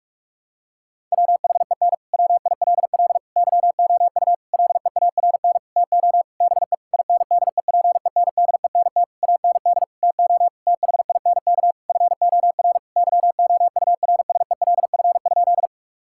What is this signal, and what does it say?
Morse code, heard as a keyed tone, about 46 wpm, 700 Hz. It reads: WHEN WILL YOU LEARN TO BE INDEPENDENT AND TO THINK FOR YOURSELF?
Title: Sentences from Top 1000 Words - Review Part 6 - 45wpm